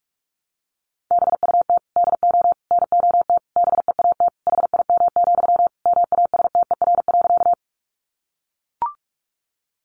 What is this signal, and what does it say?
Morse code, heard as a keyed tone, about 44 wpm, 700 Hz. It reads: BUT DO NOT BEAT HIM, MASTER.
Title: Sentences from Top 1300 Words 45wpm